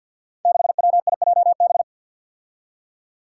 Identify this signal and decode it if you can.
Morse code, heard as a keyed tone, about 46 wpm, 700 Hz. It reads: 6WIJB
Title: Five Challenging Letter-Number Combinations 50wpm